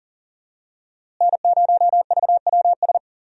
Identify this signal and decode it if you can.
Morse code, heard as a keyed tone, about 39 wpm, 700 Hz. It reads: N0VWS
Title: USA Call Signs - 5 Characters - Repeated 3x - 40wpm